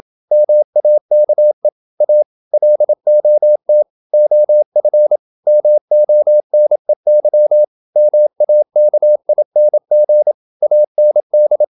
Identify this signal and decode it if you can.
Morse code, heard as a keyed tone, about 27 wpm, 600 Hz. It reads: MAKE A LOT OF MONEY MAKING AND
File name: Never 0027wpm